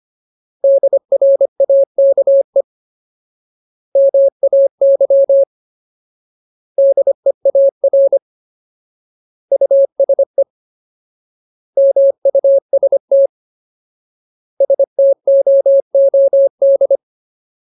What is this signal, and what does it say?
Morse code, heard as a keyed tone, about 25 wpm, 550 Hz. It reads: DRAKE MAY DEAR USE MUST STOOD
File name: Ultimate Word List / 25WPM Ultimate